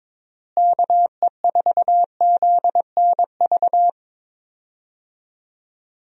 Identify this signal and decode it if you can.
Morse code, heard as a keyed tone, about 22 wpm, 700 Hz. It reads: KE4ZNV
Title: USA Call Signs - 6 Characters - Repeated 3x - 22wpm